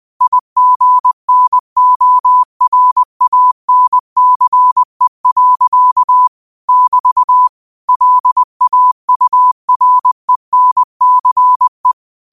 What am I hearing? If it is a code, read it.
Morse code, heard as a keyed tone, about 20 wpm, 1000 Hz. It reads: IGNORANCE. <BT> LAURENCE
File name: qotdfiles / 20wpm